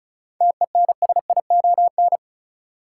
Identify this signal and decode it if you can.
Morse code, heard as a keyed tone, about 36 wpm, 700 Hz. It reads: TENSION
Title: Sentences from Top 1600 Words - New Word Drill Alt - 35wpm